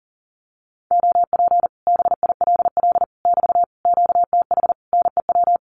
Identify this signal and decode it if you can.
Morse code, heard as a keyed tone, about 40 wpm, 700 Hz. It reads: OP BILL <BT> QTH NEW
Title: Common QSO Phrases Encore 40wpm